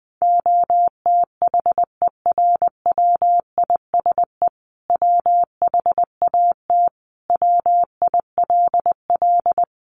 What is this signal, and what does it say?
Morse code, heard as a keyed tone, about 20 wpm, 700 Hz. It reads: OTHERWISE WHAT WILL
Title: Sentences from Top 1200 Words 20wpm